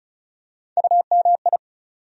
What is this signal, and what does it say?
Morse code, heard as a keyed tone, about 36 wpm, 700 Hz. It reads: UMI
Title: USA Call Sign Postfixes - 3 Letters 35wpm